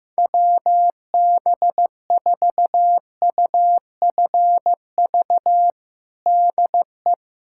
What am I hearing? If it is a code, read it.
Morse code, heard as a keyed tone, about 16 wpm, 700 Hz. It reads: WB4UFV DE
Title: USA Callsign DE Callsign - 6 Characters - 15wpm